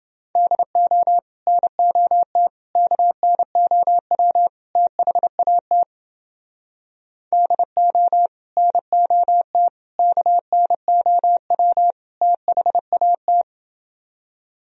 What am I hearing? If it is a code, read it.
Morse code, heard as a keyed tone, about 30 wpm, 700 Hz. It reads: DO NOT KNOW THAT DO NOT KNOW THAT